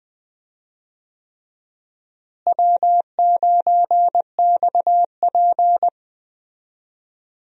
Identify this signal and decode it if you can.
Morse code, heard as a keyed tone, about 21 wpm, 700 Hz. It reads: W9XP
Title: USA Call Signs - 4 Characters - Repeated 3x - 20wpm